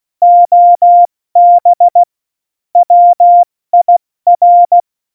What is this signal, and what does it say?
Morse code, heard as a keyed tone, about 16 wpm, 700 Hz. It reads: OB WIR
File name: Morsekurs / Witze / Witz 28. Januar 2020, 70 F80